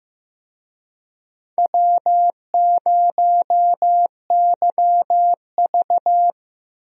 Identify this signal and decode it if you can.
Morse code, heard as a keyed tone, about 15 wpm, 700 Hz. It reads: W0YV
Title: USA Call Signs - 4 Characters - Repeated 2x - 15wpm